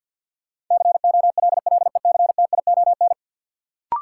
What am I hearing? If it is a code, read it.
Morse code, heard as a keyed tone, about 50 wpm, 700 Hz. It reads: COLLECTION
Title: Sentences from Top 1200 Words - New Word Drill Alt - 50wpm